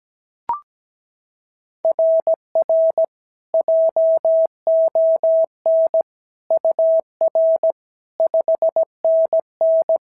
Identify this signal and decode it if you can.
Morse code, heard as a keyed tone, about 17 wpm, 690 Hz. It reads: RR JON UR 5NN
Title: SOTA QSO Practice 17wpm